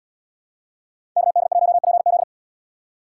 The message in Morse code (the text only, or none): KD1PZ